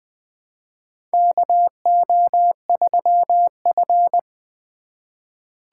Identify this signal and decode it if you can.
Morse code, heard as a keyed tone, about 21 wpm, 700 Hz. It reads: KO3F
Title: USA Call Signs - 4 Characters - 2x Character Spacing - 20wpm